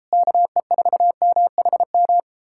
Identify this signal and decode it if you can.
Morse code, heard as a keyed tone, about 38 wpm, 700 Hz. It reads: KE4MHM